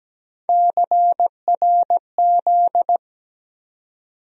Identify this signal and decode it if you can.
Morse code, heard as a keyed tone, about 17 wpm, 700 Hz. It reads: CRZ
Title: USA Call Sign Postfixes - 3 Letters 17wpm